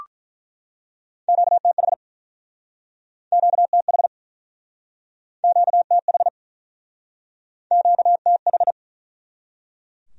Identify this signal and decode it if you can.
Morse code, heard as a keyed tone, about 46 wpm, 700 Hz. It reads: QTH QTH QTH QTH